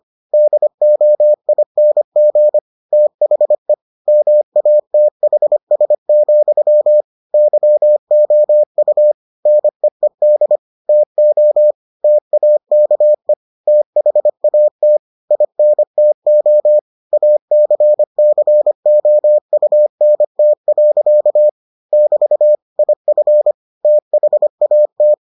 Morse code, heard as a keyed tone, about 25 wpm, 600 Hz. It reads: DOING THE MATHS, YOU NEED TO TAKE THAT INTO ACCOUNT. <BT> IF THAT